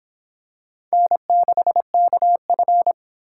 Morse code, heard as a keyed tone, about 26 wpm, 700 Hz. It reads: N6KF